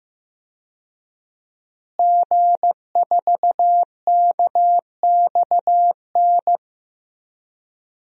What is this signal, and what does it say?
Morse code, heard as a keyed tone, about 15 wpm, 700 Hz. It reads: G4KXN
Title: DXCC-100 Call Signs - 5 Characters - Repeated 2x - 15wpm